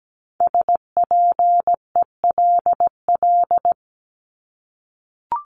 Morse code, heard as a keyed tone, about 17 wpm, 700 Hz. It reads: SPELL
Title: Top 500 Words Repeated 17wpm